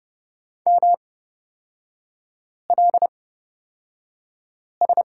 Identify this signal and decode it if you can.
Morse code, heard as a keyed tone, about 30 wpm, spaced as for 5 wpm, 700 Hz. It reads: MLS
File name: Lesson 37e - Letter x - Three Characters 30wpm